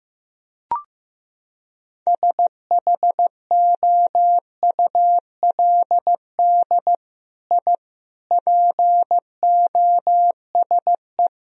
Morse code, heard as a keyed tone, about 16 wpm, 700 Hz. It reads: SHOULD I POSE
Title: Sentences from Top 1000 Words - Review Part 2 - 15wpm